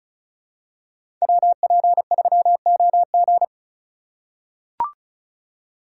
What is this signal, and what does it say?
Morse code, heard as a keyed tone, about 36 wpm, 700 Hz. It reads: WP3OG
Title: USA Call Signs - 5 Characters - 4x Character Spacing - 35wpm